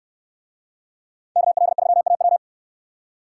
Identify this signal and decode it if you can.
Morse code, heard as a keyed tone, about 68 wpm, 700 Hz. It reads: KL3AW